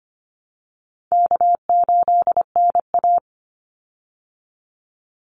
Morse code, heard as a keyed tone, about 25 wpm, 700 Hz. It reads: K8NA
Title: Easy Call Signs Repeated 2x - 25wpm